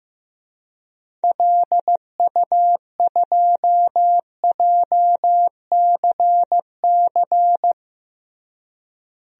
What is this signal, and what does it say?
Morse code, heard as a keyed tone, about 15 wpm, 700 Hz. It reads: LU2JCC
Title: DXCC-100 Call Signs - 6 Characters - Repeated 2x - 15wpm